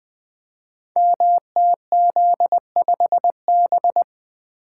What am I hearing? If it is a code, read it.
Morse code, heard as a keyed tone, about 21 wpm, 700 Hz. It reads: MTZ5B
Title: Code Groups - 5 Letter:Numbers - 20wpm